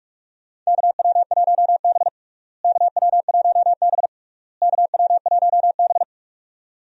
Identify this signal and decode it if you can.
Morse code, heard as a keyed tone, about 42 wpm, 700 Hz. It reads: KW1B KW1B KW1B
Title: USA Call Signs - 4 Characters - Repeated 3x - 45wpm